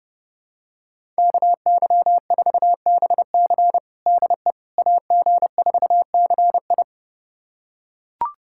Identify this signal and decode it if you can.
Morse code, heard as a keyed tone, about 30 wpm, 700 Hz. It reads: KY4BC DE AG4CI
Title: USA Callsign DE Callsign - 5 Characters - 30wpm